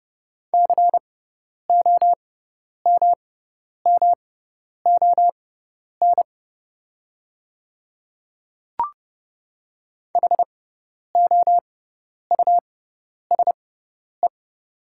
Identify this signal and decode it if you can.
Morse code, heard as a keyed tone, about 30 wpm, spaced as for 10 wpm, 700 Hz. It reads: COMMON HOUSE
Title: 6x Character Spacing - Top 500 Words - 30wpm